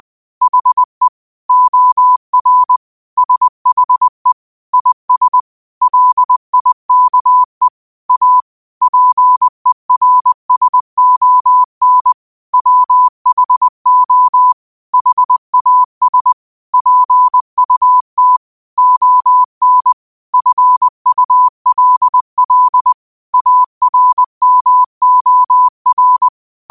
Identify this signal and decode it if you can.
Morse code, heard as a keyed tone, about 20 wpm, 1000 Hz. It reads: HE OR SHE IS LIKE A PERSON WHO HAS PUT ON FULL ARMOR